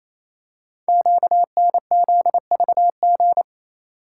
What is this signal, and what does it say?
Morse code, heard as a keyed tone, about 27 wpm, 700 Hz. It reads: QNZVG
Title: Code Groups - 5 Letters - 28wpm